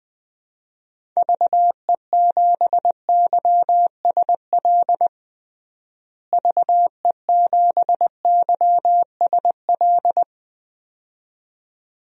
Morse code, heard as a keyed tone, about 20 wpm, 700 Hz. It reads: VE7YSL VE7YSL
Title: DXCC-100 Call Signs - 6 Characters - Repeated 2x - 20wpm